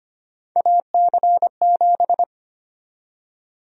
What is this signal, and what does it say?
Morse code, heard as a keyed tone, about 25 wpm, 700 Hz. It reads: AC7
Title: USA Call Sign Prefixes - 25wpm